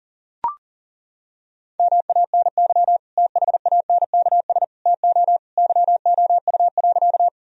Morse code, heard as a keyed tone, about 37 wpm, 700 Hz. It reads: MANY THANKS TO YOU.
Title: Sentences from Top 1000 Words - Review Part 3 - 40wpm